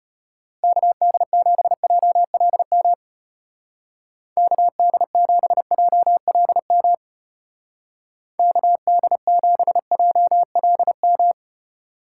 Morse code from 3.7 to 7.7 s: KD7JLM

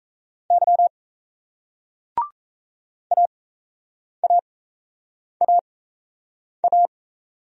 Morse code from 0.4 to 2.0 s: Y

